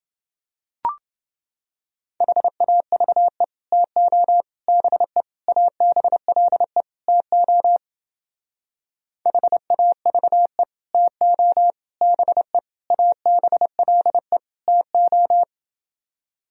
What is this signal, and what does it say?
Morse code, heard as a keyed tone, about 30 wpm, 700 Hz. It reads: HAVE TO BE ABLE TO HAVE TO BE ABLE TO